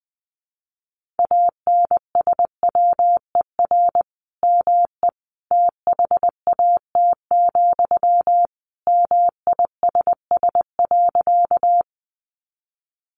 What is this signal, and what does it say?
Morse code, heard as a keyed tone, about 20 wpm, 700 Hz. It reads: ANSWER ME THAT, MISS.
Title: Sentences from Top 500 Words 20wpm